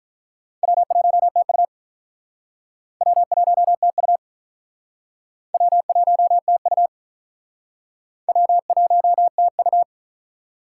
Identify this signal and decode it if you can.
Morse code, heard as a keyed tone, about 46 wpm, 700 Hz. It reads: W1TU W1TU W1TU W1TU